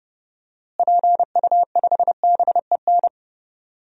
Morse code, heard as a keyed tone, about 30 wpm, 700 Hz. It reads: PU5BEN
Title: DXCC-100 Call Signs - 6 Characters - Repeated 3x - 30wpm